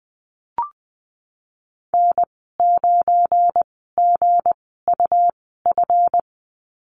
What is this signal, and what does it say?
Morse code, heard as a keyed tone, about 20 wpm, spaced as for 15 wpm, 700 Hz. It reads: N9GUF
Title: 179 Lesson 29g - Slash - Call Signs 20wpm